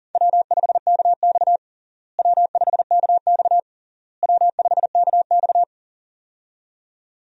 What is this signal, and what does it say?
Morse code, heard as a keyed tone, about 40 wpm, 700 Hz. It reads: W5KX W5KX W5KX
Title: USA Call Signs - 4 Characters - Repeated 3x - 40wpm